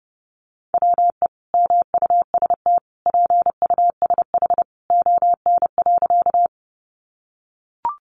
Morse code, heard as a keyed tone, about 30 wpm, 700 Hz. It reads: WE MUST PUSH ON.
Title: Sentences from Top 1300 Words 30wpm